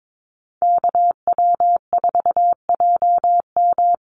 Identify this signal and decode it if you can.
Morse code, heard as a keyed tone, about 22 wpm, 700 Hz. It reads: KW4JM
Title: NA SOTA Chaser Callsigns 22wpm